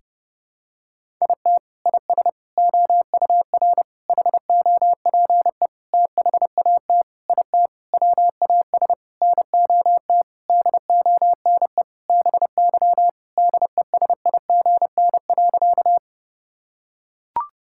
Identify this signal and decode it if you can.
Morse code, heard as a keyed tone, about 30 wpm, 700 Hz. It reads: IT IS OUR HOPE THAT IT WAS NOT DONE BY DESIGN.
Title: Sentences from Top 1000 Words - Review Part 5 - 30wpm